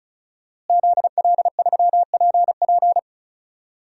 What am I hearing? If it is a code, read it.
Morse code, heard as a keyed tone, about 34 wpm, 700 Hz. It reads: ZL3PP
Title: DXCC-100 Call Signs - 5 Characters - Repeated 2x - 35wpm